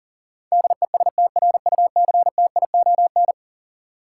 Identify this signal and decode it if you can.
Morse code, heard as a keyed tone, about 40 wpm, 700 Hz. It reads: DESTRUCTION
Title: Sets of 5 Words Encore - New Word Drill - 40wpm